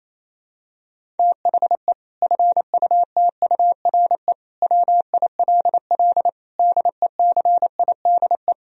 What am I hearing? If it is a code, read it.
Morse code, heard as a keyed tone, about 27 wpm, 700 Hz. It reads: THE FUTURE WILL DECIDE
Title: Sentences from Top 1000 Words - Review Part 3 - 28wpm